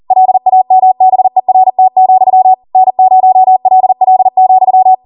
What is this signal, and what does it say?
Morse code, heard as a keyed tone, about 40 wpm, 750 Hz. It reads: LAMBERT, N0LL,